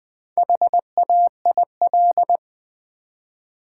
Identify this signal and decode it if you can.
Morse code, heard as a keyed tone, about 20 wpm, 700 Hz. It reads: HAIL
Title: Build A Word - 1 Letter At A Time - 4-Letter Words - 20wpm